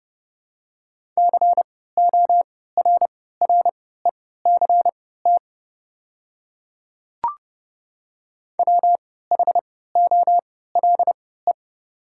Morse code, heard as a keyed tone, about 30 wpm, spaced as for 17 wpm, 690 Hz. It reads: CORRECT WHOLE
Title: 3x Character Spacing - Top 500 Words - 30wpm